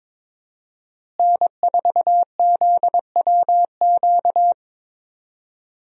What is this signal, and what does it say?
Morse code, heard as a keyed tone, about 22 wpm, 700 Hz. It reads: N4ZWQ
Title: USA Call Signs - 5 Characters - Repeated 3x - 22wpm